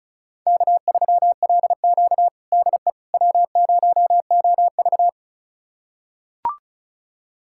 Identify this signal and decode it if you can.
Morse code, heard as a keyed tone, about 36 wpm, 700 Hz. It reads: K3LQ DE W0OV
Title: USA Callsign DE Callsign - 4 Characters - 35wpm